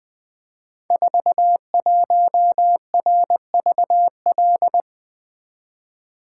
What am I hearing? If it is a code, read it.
Morse code, heard as a keyed tone, about 20 wpm, 700 Hz. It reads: 41RVL
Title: Code Groups - 5 Letter:Numbers - 20wpm